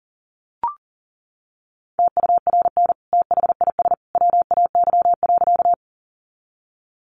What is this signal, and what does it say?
Morse code, heard as a keyed tone, about 40 wpm, 700 Hz. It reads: TURN THIS WAY.
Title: Sentences from Top 200 Words 40wpm